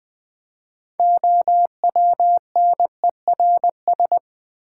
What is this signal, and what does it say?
Morse code, heard as a keyed tone, about 20 wpm, 700 Hz. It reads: OWNERS
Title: Build A Word - 3 Letters At A Time - 9-Letter Words - 20wpm